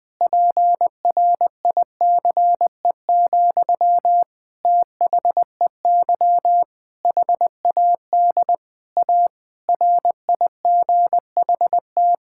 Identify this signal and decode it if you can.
Morse code, heard as a keyed tone, about 20 wpm, 700 Hz. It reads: PRICE, THEY HAD A RIGHT
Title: Sentences from Top 1000 Words - Review Part 5 - 20wpm